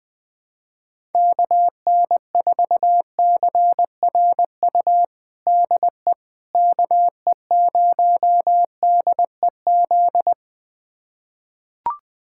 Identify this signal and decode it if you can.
Morse code, heard as a keyed tone, about 20 wpm, 700 Hz. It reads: KN4CRU DE KE0DEZ
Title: USA Callsign DE Callsign - 6 Characters - 20wpm